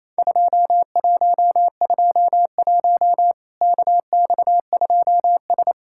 Morse code, heard as a keyed tone, about 28 wpm, 700 Hz. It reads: 2121 KX2S
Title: Worked All Europe DX Contest - QTC Exchange - 28wpm